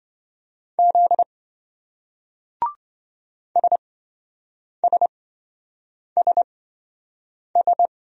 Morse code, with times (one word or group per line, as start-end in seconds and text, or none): 0.8-1.2 Z
3.6-7.9 SSSS